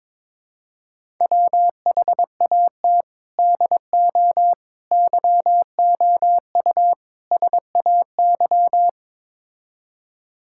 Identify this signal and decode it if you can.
Morse code, heard as a keyed tone, about 22 wpm, 700 Hz. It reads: WHAT DO YOU SAY
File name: practice / 22wpm / Build A Phrase - 5-Word Phrases - 22wpm